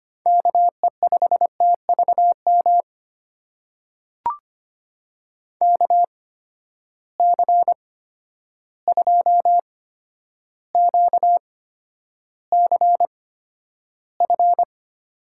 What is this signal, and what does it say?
Morse code, heard as a keyed tone, about 25 wpm, spaced as for 11 wpm, 700 Hz. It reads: KE5TVM KC2QCF
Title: USA Call Signs - 6 Characters - 8x Character Spacing - 25wpm